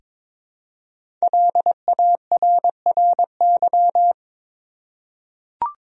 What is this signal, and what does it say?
Morse code, heard as a keyed tone, about 22 wpm, 700 Hz. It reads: LARRY
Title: USA Common Names - 22wpm